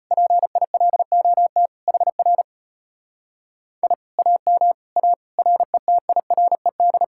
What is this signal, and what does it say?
Morse code, heard as a keyed tone, about 32 wpm, 700 Hz. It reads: PILOT HR I AM A RETIRED